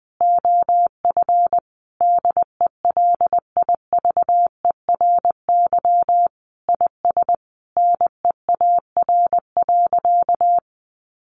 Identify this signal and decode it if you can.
Morse code, heard as a keyed tone, about 20 wpm, 700 Hz. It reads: OF DELIVERY IS NEAR.
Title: Sentences from Top 1500 Words 20wpm